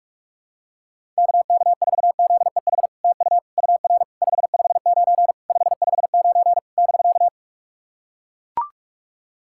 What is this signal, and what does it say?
Morse code, heard as a keyed tone, about 44 wpm, 690 Hz. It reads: KK4ZEH TU UR 559 559 <BK>
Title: SOTA QSO Practice 45wpm